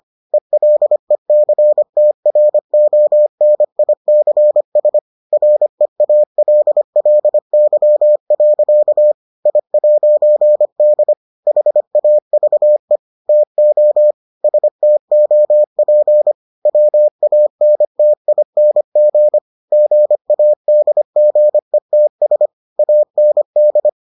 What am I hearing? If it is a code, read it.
Morse code, heard as a keyed tone, about 25 wpm, 600 Hz. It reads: ELECTRONICS REALLY. I'D HAVE TO STOP WANTING GADGETS AND